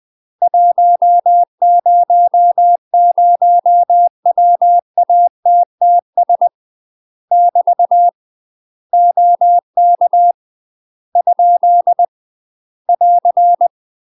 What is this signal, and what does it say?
Morse code, heard as a keyed tone, about 20 wpm, 700 Hz. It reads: 100WATTS <BT> OK ? <AR>
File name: UK MASTER QSOs TRAINING / 20WPM / UK MASTER QSOs 0005